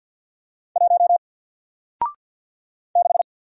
Morse code, heard as a keyed tone, about 53 wpm, 700 Hz. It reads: 1 6
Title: Single Letter-Number 50wpm